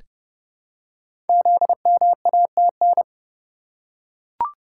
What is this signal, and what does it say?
Morse code, heard as a keyed tone, about 30 wpm, 700 Hz. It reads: ZMATN
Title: Code Groups - 5 Letters - 30wpm